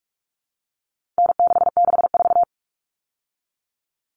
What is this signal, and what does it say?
Morse code, heard as a keyed tone, about 44 wpm, 700 Hz. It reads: N664
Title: All Cut Numbers - 4-Digit 45wpm